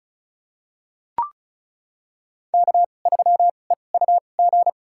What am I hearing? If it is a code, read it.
Morse code, heard as a keyed tone, about 36 wpm, spaced as for 25 wpm, 700 Hz. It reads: K3EUG
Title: USA Call Signs - 5 Characters - 2x Character Spacing - 35wpm